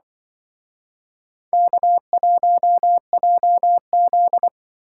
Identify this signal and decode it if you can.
Morse code, heard as a keyed tone, about 25 wpm, 700 Hz. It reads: K1JZ
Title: Speed Racing - USA 1x2 Call Signs 20wpm